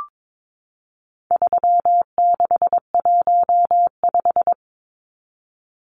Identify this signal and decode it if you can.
Morse code, heard as a keyed tone, about 22 wpm, 700 Hz. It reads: 3615
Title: Numbers - 4-Digit 22wpm